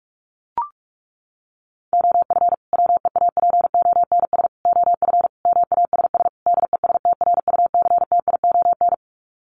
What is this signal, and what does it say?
Morse code, heard as a keyed tone, about 44 wpm, 700 Hz. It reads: OF WEAPONS OF MASS DESTRUCTION
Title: Sets of 5 Words Encore 45wpm